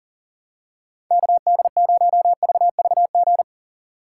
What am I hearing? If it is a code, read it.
Morse code, heard as a keyed tone, about 39 wpm, 700 Hz. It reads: KD0VVG
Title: USA Call Signs - 6 Characters - 8x Character Spacing - 40wpm